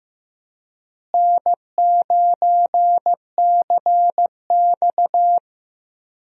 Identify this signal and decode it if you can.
Morse code, heard as a keyed tone, about 15 wpm, 700 Hz. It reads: N9CX
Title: USA Call Signs - 4 Characters - Repeated 2x - 15wpm